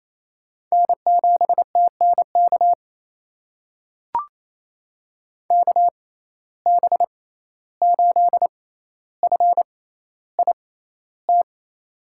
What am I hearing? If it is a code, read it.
Morse code, heard as a keyed tone, about 28 wpm, spaced as for 14 wpm, 700 Hz. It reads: N7TNK KB8FIT
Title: USA Call Signs - 6x Character Spacing - 28wpm